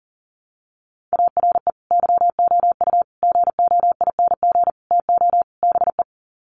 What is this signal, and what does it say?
Morse code, heard as a keyed tone, about 40 wpm, 700 Hz. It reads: ARE YOU GOING TO BE
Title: Sets of 5 Words Encore 40wpm